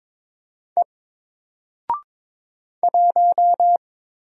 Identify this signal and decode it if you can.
Morse code, heard as a keyed tone, about 22 wpm, 700 Hz. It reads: E 1
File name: Single-Letter-Number-22wpm